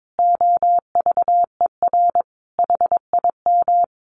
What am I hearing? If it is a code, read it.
Morse code, heard as a keyed tone, about 22 wpm, 700 Hz. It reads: OVER HIM
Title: Sentences from Top 1500 Words 22wpm